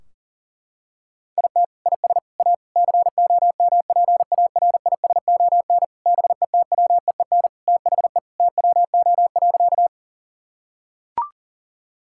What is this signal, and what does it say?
Morse code, heard as a keyed tone, about 40 wpm, 700 Hz. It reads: IT IS A COMPARISON BETWEEN THE TWO.
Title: Sentences from Top 1600 Words 40wpm